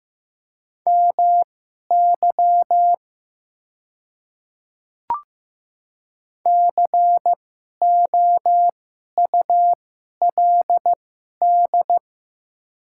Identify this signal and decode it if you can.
Morse code, heard as a keyed tone, about 15 wpm, spaced as for 11 wpm, 700 Hz. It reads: MY COULD
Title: 2x Character Spacing - Top 100 Words - 15wpm